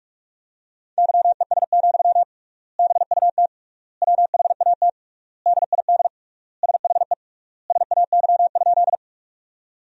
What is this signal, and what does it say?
Morse code, heard as a keyed tone, about 46 wpm, 700 Hz. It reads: YES, BUT WHAT DID SHE SAY?
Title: 3x Word-Spacing - Sentences from Top 500 Words - 45wpm